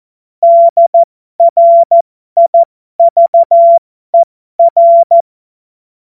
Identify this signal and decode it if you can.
Morse code, heard as a keyed tone, about 14 wpm, spaced as for 12 wpm, 680 Hz. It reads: DRIVER
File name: Morsekurs / Witze